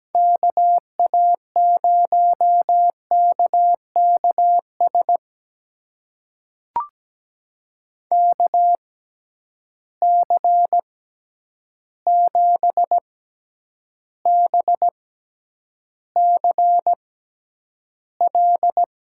Morse code, heard as a keyed tone, about 17 wpm, spaced as for 9 wpm, 700 Hz. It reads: KA0KKS KC7BCL
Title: USA Call Signs - 6x Character Spacing - 17wpm